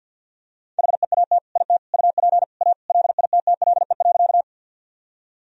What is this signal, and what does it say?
Morse code, heard as a keyed tone, about 50 wpm, 700 Hz. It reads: HEAT IT UP A LITTLE.
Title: Sentences from Top 1000 Words - Review Part 3 - 50wpm